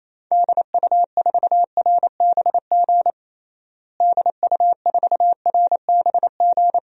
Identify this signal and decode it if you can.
Morse code, heard as a keyed tone, about 28 wpm, 700 Hz. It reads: DU4RBG DU4RBG
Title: DXCC-100 Call Signs - 6 Characters - Repeated 3x - 28wpm